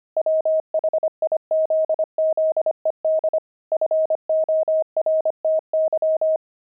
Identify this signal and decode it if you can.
Morse code, heard as a keyed tone, about 25 wpm, 630 Hz. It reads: WHIZZED FORTY
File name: pangram-fs-25-cs-25-javelins